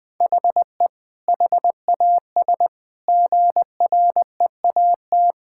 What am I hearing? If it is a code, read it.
Morse code, heard as a keyed tone, about 21 wpm, 700 Hz. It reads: HE HAS GREAT
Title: Sentences from Top 1000 Words - Review Part 6 - 20wpm